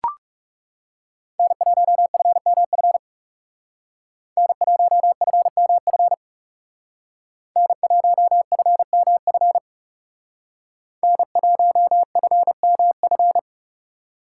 Morse code, heard as a keyed tone, about 36 wpm, 700 Hz. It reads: N1FMF N1FMF N1FMF N1FMF
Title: Speed Racing - USA Call Signs 30wpm